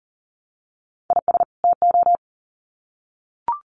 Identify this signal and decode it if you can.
Morse code, heard as a keyed tone, about 40 wpm, 700 Hz. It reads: IS TO
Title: Sets of 2 Words 40wpm